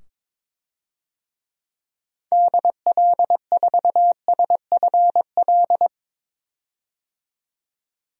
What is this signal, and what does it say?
Morse code, heard as a keyed tone, about 22 wpm, 700 Hz. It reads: DL4SFL
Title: DXCC-100 Call Signs - 6 Characters - 22wpm